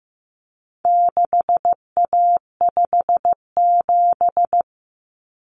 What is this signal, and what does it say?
Morse code, heard as a keyed tone, about 15 wpm, 700 Hz. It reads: 6A57
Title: Common Cut Numbers - 4-Digit 15wpm